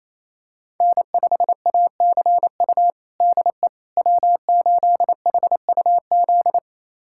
Morse code, heard as a keyed tone, about 28 wpm, 700 Hz. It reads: N5ACU DE W8HUZ
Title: USA Callsign DE Callsign - 5 Characters - 28wpm